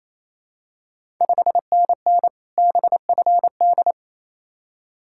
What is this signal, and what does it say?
Morse code, heard as a keyed tone, about 28 wpm, 700 Hz. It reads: 5NN BFD
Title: Ontario QSO Party Exchange 28wpm